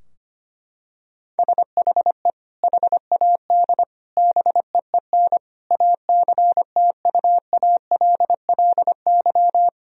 Morse code, heard as a keyed tone, about 25 wpm, 700 Hz. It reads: SHE HAD BEEN ACTUALLY
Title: Sentences from Top 1000 Words - Review Part 4 - 25wpm